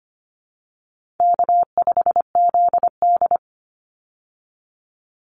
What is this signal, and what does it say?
Morse code, heard as a keyed tone, about 25 wpm, 700 Hz. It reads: K5ZD
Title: Easy Call Signs Repeated 3x - 25wpm